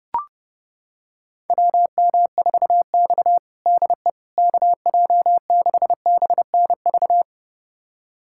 Morse code, heard as a keyed tone, about 30 wpm, 700 Hz. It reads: WM4X DE KJ6BNV
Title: USA Callsign DE Callsign 30wpm